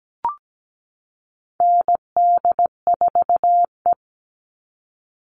Morse code, heard as a keyed tone, about 17 wpm, 700 Hz. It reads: ND4E
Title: All Cut Numbers - 4-Digit 17wpm